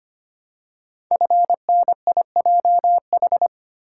Morse code, heard as a keyed tone, about 25 wpm, 700 Hz. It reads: FNIJH
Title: Code Groups - 5 Letters - 25wpm